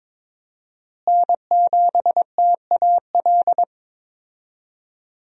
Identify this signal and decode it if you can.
Morse code, heard as a keyed tone, about 21 wpm, 700 Hz. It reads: N7TAL
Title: USA Call Signs - 5 Characters - Repeated 2x - 22wpm